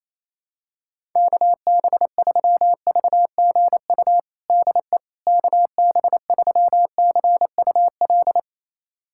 Morse code, heard as a keyed tone, about 28 wpm, 700 Hz. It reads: KB3VGU DE KB3CUL
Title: USA Callsign DE Callsign 28wpm